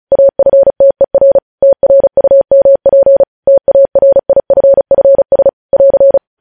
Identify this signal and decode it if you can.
Morse code, heard as a keyed tone, about 36 wpm, 550 Hz. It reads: AFTER TRUMP TARIFFS <AR>
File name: podcast / News headlines-1x-35x35